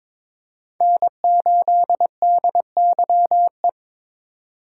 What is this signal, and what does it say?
Morse code, heard as a keyed tone, about 21 wpm, 700 Hz. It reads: N8DYE
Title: USA Call Signs - 5 Characters - 4x Character Spacing - 22wpm